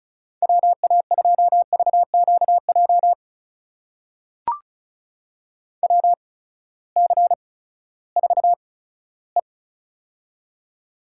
Morse code, heard as a keyed tone, about 36 wpm, spaced as for 18 wpm, 700 Hz. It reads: WA2VQJ WC4E
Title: USA Call Signs - 8x Character Spacing - 35wpm